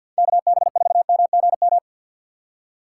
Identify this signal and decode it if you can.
Morse code, heard as a keyed tone, about 52 wpm, 700 Hz. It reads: KB4MGM
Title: USA Call Signs - 6 Characters - 3x Character Spacing - 50wpm